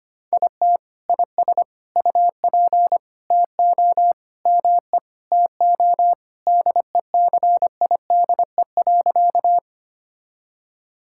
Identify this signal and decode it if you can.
Morse code, heard as a keyed tone, about 25 wpm, 700 Hz. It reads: IT IS UP TO ME TO DECIDE.
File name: practice / 25wpm / Sentences with 2-Word Phrases - 25wpm